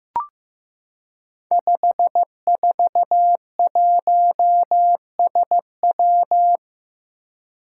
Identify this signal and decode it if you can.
Morse code, heard as a keyed tone, about 15 wpm, 700 Hz. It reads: 541SW
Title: Five Challenging Letter-Number Combinations 15wpm